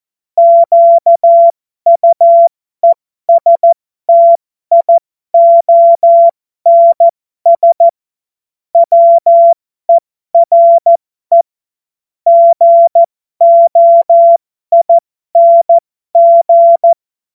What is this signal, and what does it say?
Morse code, heard as a keyed tone, about 14 wpm, spaced as for 12 wpm, 680 Hz. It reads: QUESTIONS WERE GOING